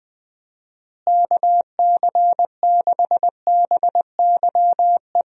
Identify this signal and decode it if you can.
Morse code, heard as a keyed tone, about 20 wpm, 700 Hz. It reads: KC6BYE
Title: USA Call Signs - 6 Characters - Repeated 3x - 20wpm